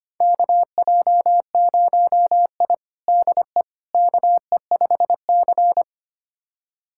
Morse code, heard as a keyed tone, about 25 wpm, 700 Hz. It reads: KJ0I DE KE5C
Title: USA Callsign DE Callsign - 4 Characters - 25wpm